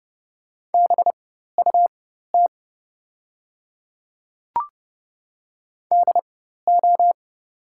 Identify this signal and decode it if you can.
Morse code, heard as a keyed tone, about 30 wpm, spaced as for 14 wpm, 700 Hz. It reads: BUT DO
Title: 4x Character Spacing - Top 200 Words - 30wpm